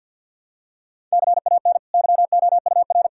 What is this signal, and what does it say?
Morse code, heard as a keyed tone, about 46 wpm, 700 Hz. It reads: CAN YOUR
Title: Sentences from Top 1000 Words - Review Part 5 - 50wpm